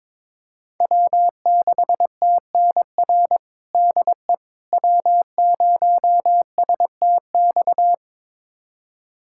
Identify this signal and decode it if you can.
Morse code, heard as a keyed tone, about 22 wpm, 700 Hz. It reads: W6TNR DE W0STX
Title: USA Callsign DE Callsign - 5 Characters - 22wpm